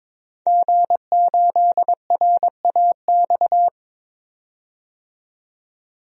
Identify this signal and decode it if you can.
Morse code, heard as a keyed tone, about 21 wpm, 700 Hz. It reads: G8RAX
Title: DXCC-100 Call Signs - 5 Characters - 22wpm